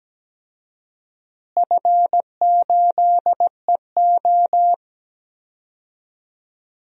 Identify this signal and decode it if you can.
Morse code, heard as a keyed tone, about 17 wpm, 700 Hz. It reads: F8EO
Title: DXCC-100 Call Signs - 4 Characters - Repeated 3x - 17wpm